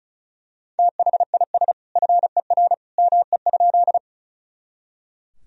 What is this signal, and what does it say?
Morse code, heard as a keyed tone, about 36 wpm, 700 Hz. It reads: THIS FER ME?
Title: Sentences from Top 200 Words and Abbreviations 35wpm